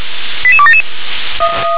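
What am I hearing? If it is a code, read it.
Five-tone selective call (ZVEI), heard as quick runs of five tones, tones of about 70 ms. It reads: 88277
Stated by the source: Received on SDR as ZVEI five-tone selective call.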